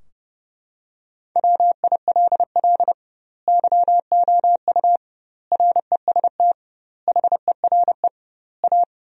Morse code, heard as a keyed tone, about 32 wpm, 700 Hz. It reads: WILL YOU REST HERE A
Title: 2x Word-Spacing - Sentences from Top 500 Words - 30wpm